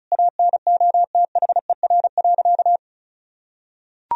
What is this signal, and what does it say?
Morse code, heard as a keyed tone, about 34 wpm, 690 Hz. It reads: ANOTHER.
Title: Sentences from Top 1000 Words - Review Part 1 - 35wpm